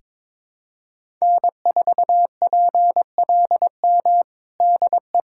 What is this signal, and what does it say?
Morse code, heard as a keyed tone, about 22 wpm, 700 Hz. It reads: N4PLM DE
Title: USA Callsign DE Callsign 22wpm